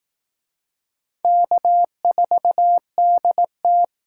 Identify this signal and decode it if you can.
Morse code, heard as a keyed tone, about 19 wpm, 700 Hz. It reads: K4DT
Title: Speed Racing - USA 1x2 Call Signs 15wpm